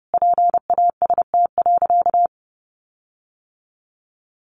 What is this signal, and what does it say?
Morse code, heard as a keyed tone, about 30 wpm, 700 Hz. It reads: PAST.
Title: Sentences from Top 1300 Words 30wpm